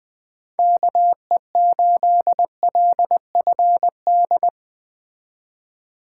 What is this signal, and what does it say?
Morse code, heard as a keyed tone, about 21 wpm, 700 Hz. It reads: KE8LFD
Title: Speed Racing - USA Call Signs 17wpm